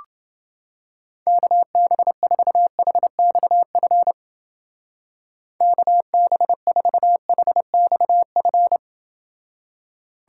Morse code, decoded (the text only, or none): KB4HXF KB4HXF